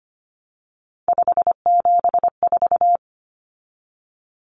Morse code, heard as a keyed tone, about 25 wpm, 700 Hz. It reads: 574
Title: Numbers - 3-Digit 25wpm